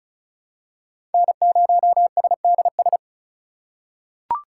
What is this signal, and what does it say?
Morse code, heard as a keyed tone, about 36 wpm, 700 Hz. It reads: N0SDS
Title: USA Call Signs - 5 Characters - 3x Character Spacing - 35wpm